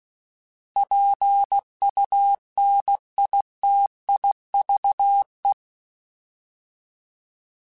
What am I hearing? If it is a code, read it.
Morse code, heard as a keyed tone, about 16 wpm, 800 Hz. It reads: PUNITIVE
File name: Alphabetical / Alphabetical mp3s / Ps @15